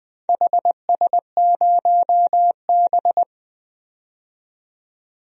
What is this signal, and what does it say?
Morse code, heard as a keyed tone, about 20 wpm, 700 Hz. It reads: HS0B
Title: DXCC-100 Call Signs - 4 Characters - 20wpm